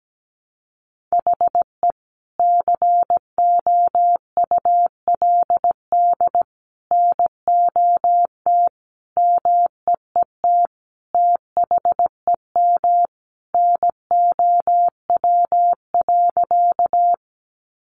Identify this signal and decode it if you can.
Morse code, heard as a keyed tone, about 17 wpm, 700 Hz. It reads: HE COULD NOT MEET THEM NOW.
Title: Sentences from Top 1100 Words 17wpm